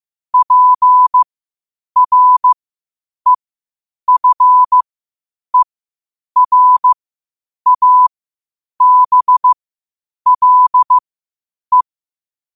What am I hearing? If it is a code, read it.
Morse code, heard as a keyed tone, about 15 wpm, spaced as for 8 wpm, 1000 Hz. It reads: PREFERABLE
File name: qotdfiles / 8wpm